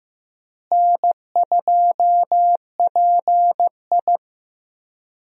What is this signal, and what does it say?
Morse code, heard as a keyed tone, about 14 wpm, 700 Hz. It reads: N2PI
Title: USA Call Signs - 4 Characters - Repeated 3x - 15wpm